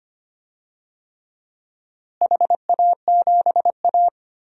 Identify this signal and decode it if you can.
Morse code, heard as a keyed tone, about 25 wpm, 700 Hz. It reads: HA7A
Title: DXCC-100 Call Signs - 4 Characters - Repeated 3x - 25wpm